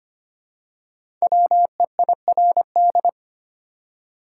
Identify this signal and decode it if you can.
Morse code, heard as a keyed tone, about 25 wpm, 700 Hz. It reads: WEIRD
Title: Five Letter Words - 25wpm